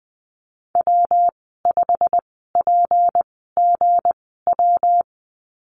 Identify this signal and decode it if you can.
Morse code, heard as a keyed tone, about 20 wpm, spaced as for 14 wpm, 700 Hz. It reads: W5PGW
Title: 167 Lesson 27g - Number 7 - Call Signs 20wpm